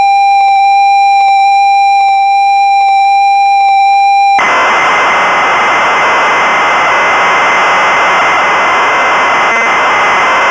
Computer data on cassette.